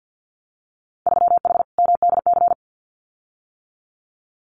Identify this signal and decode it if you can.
Morse code, heard as a keyed tone, about 50 wpm, 700 Hz. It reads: 3H MDC
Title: Winter Field Day Exchanges Encore 50wpm